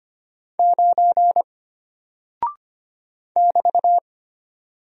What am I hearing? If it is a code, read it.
Morse code, heard as a keyed tone, about 25 wpm, 700 Hz. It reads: 9 <BT>
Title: Single Character (Letter:Number:Punctuation:Prosign) Alt 25wpm